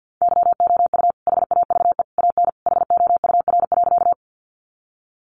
Morse code, heard as a keyed tone, about 48 wpm, 700 Hz. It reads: YOU HAVE AN HOUR.
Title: Sentences from Top 400 Words 50wpm